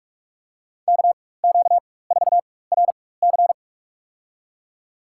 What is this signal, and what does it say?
Morse code, heard as a keyed tone, about 47 wpm, spaced as for 21 wpm, 690 Hz. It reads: KQ4RC
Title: USA Call Signs - 5 Characters - 4x Character Spacing - 45wpm